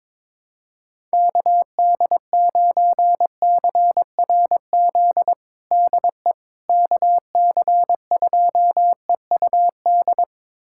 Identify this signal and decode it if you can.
Morse code, heard as a keyed tone, about 22 wpm, 700 Hz. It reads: KD9CRZ DE KC2EUD
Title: USA Callsign DE Callsign 22wpm